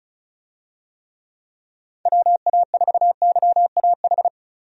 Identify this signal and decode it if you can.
Morse code, heard as a keyed tone, about 36 wpm, 700 Hz. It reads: WA4YAH
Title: USA Call Signs - 6 Characters - Repeated 2x - 35wpm